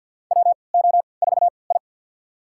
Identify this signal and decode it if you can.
Morse code, heard as a keyed tone, about 52 wpm, spaced as for 29 wpm, 700 Hz. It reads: WO4I
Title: USA Call Signs - 4 Characters - 3x Character Spacing - 50wpm